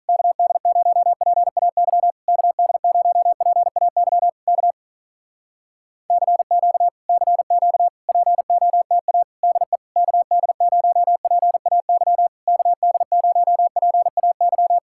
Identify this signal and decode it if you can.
Morse code, heard as a keyed tone, about 46 wpm, 700 Hz. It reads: KD0PAY KD0PAY K CQ CQ POTA DE KD0PAY KD0PAY